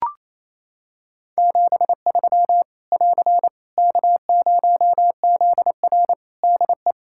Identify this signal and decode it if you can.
Morse code, heard as a keyed tone, about 27 wpm, 700 Hz. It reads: 73 <AR> K0ZR DE
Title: K1USN SST QSO Practice 28wpm